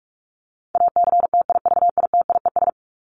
Morse code, heard as a keyed tone, about 44 wpm, 700 Hz. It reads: ACTIVITIES
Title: ES:S Suffix - Top 100 Words with Base Words 45wpm